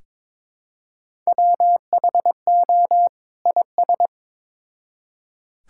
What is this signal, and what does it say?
Morse code, heard as a keyed tone, about 22 wpm, 700 Hz. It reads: WHO IS
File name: Build A Phrase - 2-Word Phrases - 22wpm